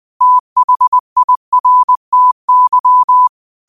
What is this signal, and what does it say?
Morse code, heard as a keyed tone, about 20 wpm, 1000 Hz. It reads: THIRTY